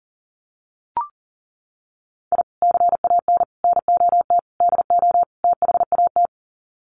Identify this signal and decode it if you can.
Morse code, heard as a keyed tone, about 37 wpm, 700 Hz. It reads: I CAN NOT DO THAT
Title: Sets of 5 Words Encore2 40wpm